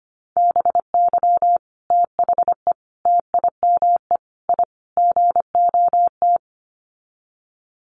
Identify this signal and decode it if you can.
Morse code, heard as a keyed tone, about 25 wpm, 700 Hz. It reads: BY THE TIME I GOT
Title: Sets of 5 Words Encore2 25wpm